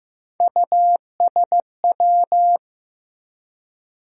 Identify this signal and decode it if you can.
Morse code, heard as a keyed tone, about 15 wpm, 700 Hz. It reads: USW
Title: USA Call Sign Postfixes - 3 Letters 15wpm